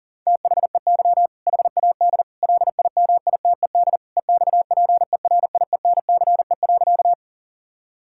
Morse code, heard as a keyed tone, about 37 wpm, 700 Hz. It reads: THEY HAD LIMITED EXPERIENCE.
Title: Sentences from Top 1000 Words - Review Part 6 - 40wpm